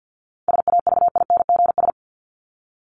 Morse code, heard as a keyed tone, about 50 wpm, 700 Hz. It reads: SAVINGS
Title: Top 1001-1250 Words 50wpm